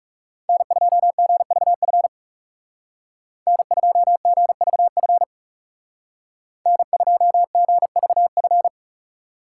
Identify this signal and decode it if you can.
Morse code, heard as a keyed tone, about 40 wpm, 700 Hz. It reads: N2GVF N2GVF N2GVF